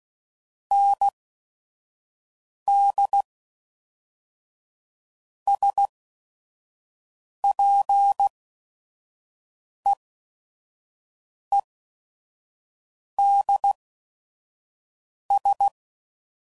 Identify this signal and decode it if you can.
Morse code, heard as a keyed tone, about 16 wpm, spaced as for 5 wpm, 800 Hz. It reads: ND SPEEDS
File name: CW Code Groups @5WPM / Weather Sets / Weather 4 10